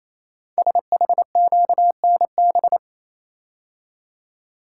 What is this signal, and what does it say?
Morse code, heard as a keyed tone, about 28 wpm, 700 Hz. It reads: SHQNB
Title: Code Groups - 5 Letters - 28wpm